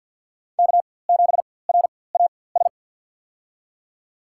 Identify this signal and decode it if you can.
Morse code, heard as a keyed tone, about 52 wpm, spaced as for 23 wpm, 700 Hz. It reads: K7RAS
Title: USA Call Signs - 5 Characters - 4x Character Spacing - 50wpm